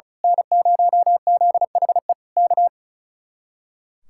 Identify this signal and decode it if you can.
Morse code, heard as a keyed tone, about 32 wpm, 700 Hz. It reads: N0ZHE K
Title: Speed-Racing Common QSO Phrases Encore - 35wpm